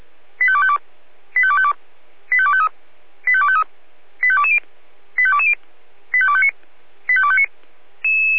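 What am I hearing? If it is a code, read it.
Five-tone selective call (ZVEI), heard as quick runs of five tones, tones of about 70 ms. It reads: 85252 85252 85253 85253 85220 85220 85259 85259
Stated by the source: ZVEI five-tone selective call received with an SDR.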